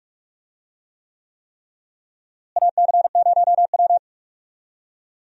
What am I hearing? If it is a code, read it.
Morse code, heard as a keyed tone, about 46 wpm, 700 Hz. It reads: AC0W
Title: USA Call Signs - 4 Characters - Repeated 3x - 45wpm